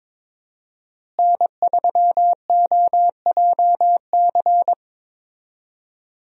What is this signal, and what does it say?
Morse code, heard as a keyed tone, about 22 wpm, 700 Hz. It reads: N3OJC
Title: USA Call Signs - 5 Characters - Repeated 2x - 22wpm